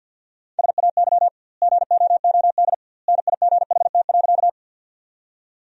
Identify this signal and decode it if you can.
Morse code, heard as a keyed tone, about 52 wpm, 700 Hz. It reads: SAY GOOD NIGHT.
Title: 2x Word-Spacing - Sentences from Top 300 Words - 50wpm